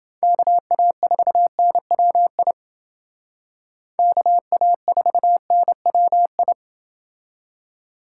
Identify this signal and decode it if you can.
Morse code, heard as a keyed tone, about 28 wpm, 700 Hz. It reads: KA4NWI KA4NWI